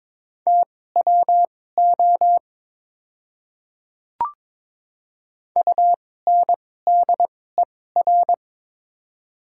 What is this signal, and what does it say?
Morse code, heard as a keyed tone, about 22 wpm, spaced as for 16 wpm, 700 Hz. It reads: TWO UNDER
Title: 2x Character Spacing - Top 300 Words - 22wpm